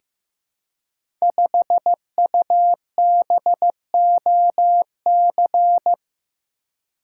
Morse code, heard as a keyed tone, about 15 wpm, 700 Hz. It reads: 5UBOC
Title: Code Groups - 5 Letter:Numbers - 15wpm